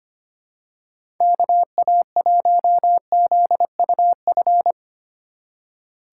Morse code, heard as a keyed tone, about 25 wpm, 700 Hz. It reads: KA1ZUF
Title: USA Call Signs - 6 Characters - 4x Character Spacing - 25wpm